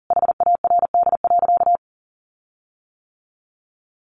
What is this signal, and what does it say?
Morse code, heard as a keyed tone, about 40 wpm, 700 Hz. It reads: HARD.
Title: Sentences from Top 1300 Words 40wpm